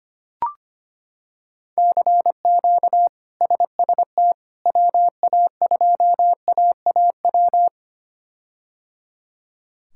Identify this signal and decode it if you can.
Morse code, heard as a keyed tone, about 25 wpm, 700 Hz. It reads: CQ SST WA2AAW
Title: K1USN SST QSO Practice - 25wpm